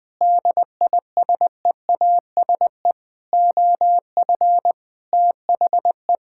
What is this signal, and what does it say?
Morse code, heard as a keyed tone, about 20 wpm, 700 Hz. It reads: DISEASE OF THE
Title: Sentences from Top 1000 Words - Review Part 6 - 20wpm